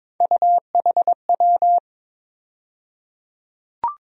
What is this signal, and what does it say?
Morse code, heard as a keyed tone, about 22 wpm, 700 Hz. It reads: UHW
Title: USA Call Sign Postfixes - 3 Letters 22wpm